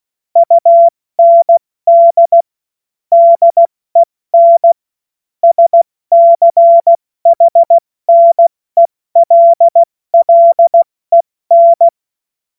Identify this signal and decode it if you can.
Morse code, heard as a keyed tone, about 16 wpm, spaced as for 14 wpm, 680 Hz. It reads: UND DEN SCHNELLEN
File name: Witz-21-Oktober-2025-70-F80